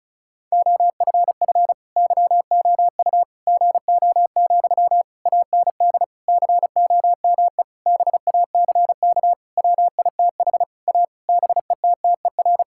Morse code, heard as a keyed tone, about 36 wpm, 700 Hz. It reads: OFF YOU GO, AND COME BACK WITH A BETTER
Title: Sentences from Top 1000 Words - Review Part 6 - 35wpm